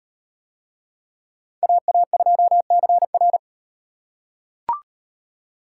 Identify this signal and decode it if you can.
Morse code, heard as a keyed tone, about 37 wpm, 700 Hz. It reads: AA2CR